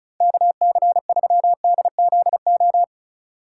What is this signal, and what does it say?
Morse code, heard as a keyed tone, about 36 wpm, 700 Hz. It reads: KC3DZO
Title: USA Call Signs - 6 Characters - 4x Character Spacing - 35wpm